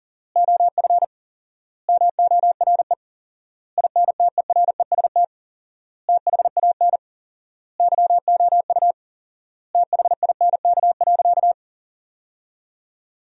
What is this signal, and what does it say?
Morse code, heard as a keyed tone, about 40 wpm, 700 Hz. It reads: OF MORE INTEREST THAN YOU THINK.
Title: 4x Word-Spacing - Sentences from Top 400 Words - 40wpm